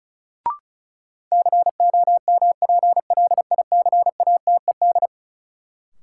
Morse code, heard as a keyed tone, about 36 wpm, 700 Hz. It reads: COMPLICATED
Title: Sentences from Top 1200 Words - New Word Drill Alt - 35wpm